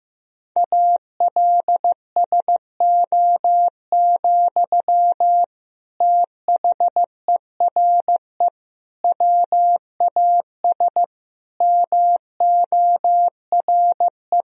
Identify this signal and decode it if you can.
Morse code, heard as a keyed tone, about 15 wpm, 700 Hz. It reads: ALSO, THERE WAS MORE
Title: Sentences from Top 1000 Words - Review Part 1 - 15wpm